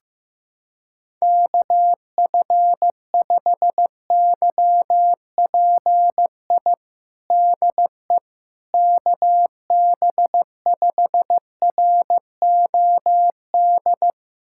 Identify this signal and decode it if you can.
Morse code, heard as a keyed tone, about 15 wpm, 700 Hz. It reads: KF5YPI DE KB5ROD
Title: USA Callsign DE Callsign 15wpm